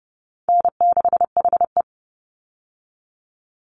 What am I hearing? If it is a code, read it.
Morse code, heard as a keyed tone, about 30 wpm, 700 Hz. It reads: N6HE
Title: Call Signs - 30wpm